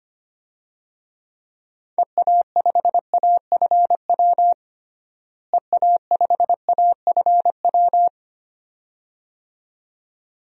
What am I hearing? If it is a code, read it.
Morse code, heard as a keyed tone, about 25 wpm, 700 Hz. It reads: EA5AFW EA5AFW
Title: DXCC-100 Call Signs - 6 Characters - Repeated 2x - 25wpm